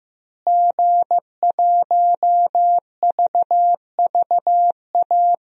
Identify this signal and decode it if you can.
Morse code, heard as a keyed tone, about 15 wpm, 700 Hz. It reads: G1VVA
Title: DXCC-100 Call Signs - 5 Characters - Repeated 2x - 15wpm